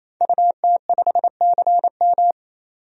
Morse code, heard as a keyed tone, about 27 wpm, 700 Hz. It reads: UT5CM
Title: DXCC-100 Call Signs - 5 Characters - 28wpm